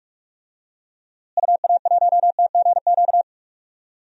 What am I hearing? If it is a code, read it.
Morse code, heard as a keyed tone, about 43 wpm, 700 Hz. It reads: UA1TGQ
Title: DXCC-100 Call Signs - 6 Characters - Repeated 2x - 45wpm